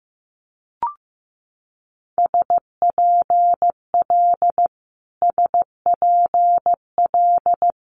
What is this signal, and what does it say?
Morse code, heard as a keyed tone, about 15 wpm, 700 Hz. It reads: SPL SPL
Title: Multi-letter Phonemes Repeated 3x 15wpm